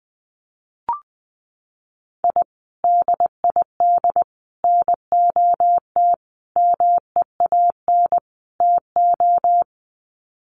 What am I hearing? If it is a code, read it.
Morse code, heard as a keyed tone, about 20 wpm, 700 Hz. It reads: I DID NOT MEAN TO
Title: Sets of 5 Words 20wpm